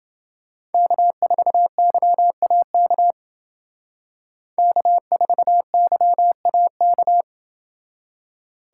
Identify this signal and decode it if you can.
Morse code, heard as a keyed tone, about 29 wpm, 700 Hz. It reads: K4YAK K4YAK